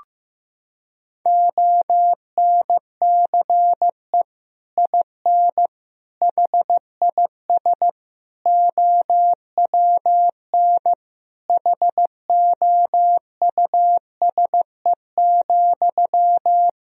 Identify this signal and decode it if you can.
Morse code, heard as a keyed tone, about 15 wpm, 700 Hz. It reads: ONCE IN HIS OWN HOUSE,
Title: Sentences from Top 1000 Words - Review Part 2 - 15wpm